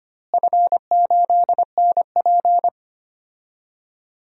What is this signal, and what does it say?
Morse code, heard as a keyed tone, about 25 wpm, 700 Hz. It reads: F8NP
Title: DXCC-100 Call Signs - 4 Characters - Repeated 3x - 25wpm